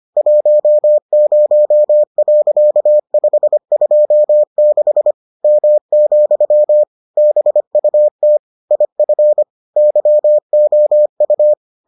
Morse code, heard as a keyed tone, about 25 wpm, 600 Hz. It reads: 10.526 M, BUT IF YOU